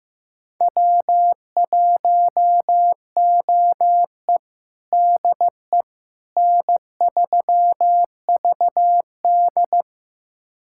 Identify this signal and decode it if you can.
Morse code, heard as a keyed tone, about 15 wpm, 700 Hz. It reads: W1OE DE N3VD
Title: USA Callsign DE Callsign - 4 Characters - 15wpm